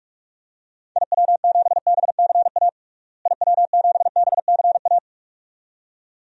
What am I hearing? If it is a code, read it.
Morse code, heard as a keyed tone, about 46 wpm, 700 Hz. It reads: IW7BCA IW7BCA
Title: DXCC-100 Call Signs - 6 Characters - Repeated 2x - 45wpm